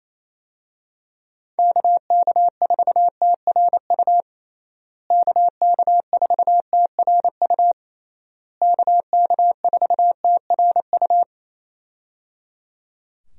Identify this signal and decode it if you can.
Morse code, heard as a keyed tone, about 27 wpm, 700 Hz. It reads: KK4TRU KK4TRU KK4TRU
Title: USA Call Signs - 6 Characters - Repeated 3x - 28wpm